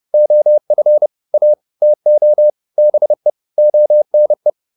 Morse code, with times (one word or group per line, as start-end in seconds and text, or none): 0.1-1.1 OF
1.3-1.5 A
1.8-2.5 TO
2.8-3.3 BE
3.6-4.5 ONE